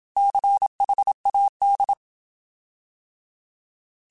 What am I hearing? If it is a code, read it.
Morse code, heard as a keyed tone, about 27 wpm, 800 Hz. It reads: CHAD
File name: Alphabetical / Alphabetical mp3s / Cs @25